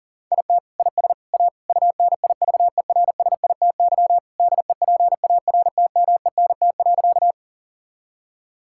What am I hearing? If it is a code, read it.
Morse code, heard as a keyed tone, about 40 wpm, 700 Hz. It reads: IT IS A UNIVERSITY DEPARTMENT.
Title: Sentences from Top 1000 Words - Review Part 5 - 40wpm